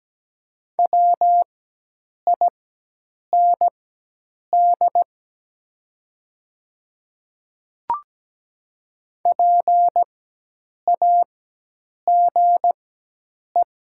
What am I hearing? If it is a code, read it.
Morse code, heard as a keyed tone, about 17 wpm, spaced as for 8 wpm, 700 Hz. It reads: WIND PAGE
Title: 4x Character Spacing - Top 500 Words - 17wpm